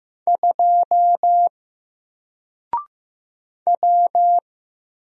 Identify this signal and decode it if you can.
Morse code, heard as a keyed tone, about 15 wpm, 700 Hz. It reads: 2 W
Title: Single Letter-Number Alt 15wpm